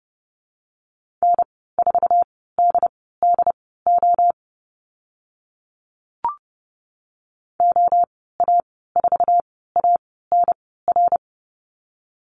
Morse code, heard as a keyed tone, about 30 wpm, spaced as for 17 wpm, 700 Hz. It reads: N4DDO OA4ANR
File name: Lesson 15g - Number 5 - Call Signs 30wpm